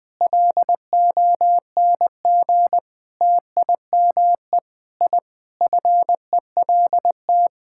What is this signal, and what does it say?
Morse code, heard as a keyed tone, about 20 wpm, 700 Hz. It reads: LONG TIME I FELT
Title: Sentences with 8-Word Phrases - Spoken Beginning - 20wpm